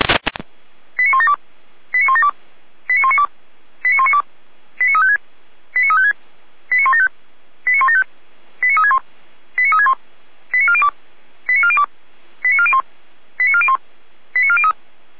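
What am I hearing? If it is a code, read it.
Five-tone selective call (ZVEI), heard as quick runs of five tones, tones of about 80 ms. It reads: 89172 89172 89192 89192 89356 89356 89175 89175 89361 89361 89402 89402 89401 89401 89403
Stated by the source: ZVEI five-tone selective call 89108 received by SDR, logged on the feuerwehr-orlamuende.org audio server.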